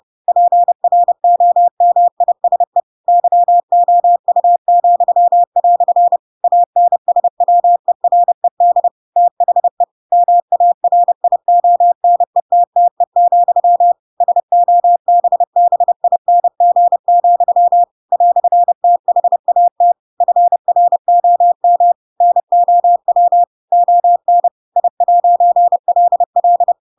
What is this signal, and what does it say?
Morse code, heard as a keyed tone, about 30 wpm, 700 Hz. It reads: PROMISE YOU," ANSWERED THE MARIONETTE, SOBBING, "THAT FROM NOW ON I'LL
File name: Pinocchio / Pinocchio ch8 30 30 ik1hge